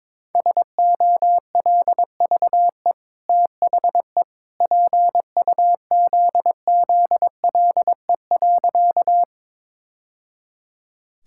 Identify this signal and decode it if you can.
Morse code, heard as a keyed tone, about 22 wpm, 700 Hz. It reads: SOLVE THE PUZZLE.
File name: practice / 22wpm / Easy Sentences with Hints - 22wpm